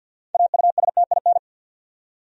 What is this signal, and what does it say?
Morse code, heard as a keyed tone, about 53 wpm, 700 Hz. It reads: AUSTIN
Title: USA Common Names - 50wpm